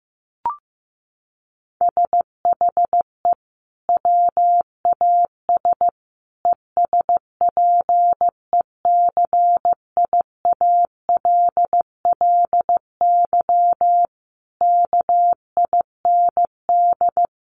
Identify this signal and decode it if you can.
Morse code, heard as a keyed tone, about 15 wpm, 700 Hz. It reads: SHE WAS ESPECIALLY KIND
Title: Sentences from Top 700 Words 15wpm